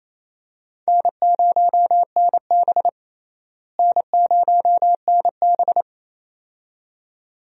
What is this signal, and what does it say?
Morse code, heard as a keyed tone, about 28 wpm, 700 Hz. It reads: N0NB N0NB
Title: USA Call Signs - 4 Characters - Repeated 2x - 28wpm